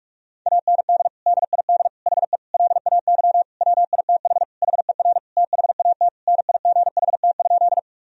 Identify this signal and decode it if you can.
Morse code, heard as a keyed tone, about 43 wpm, 700 Hz. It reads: AND DID HE LAY WITH HER THAT NIGHT?
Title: Sentences from Top 1000 Words - Review Part 2 - 45wpm